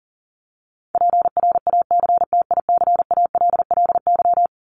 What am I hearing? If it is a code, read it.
Morse code, heard as a keyed tone, about 40 wpm, 700 Hz. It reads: PRACTICALLY
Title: LY Suffix - Top 100 Words with Base Words 40wpm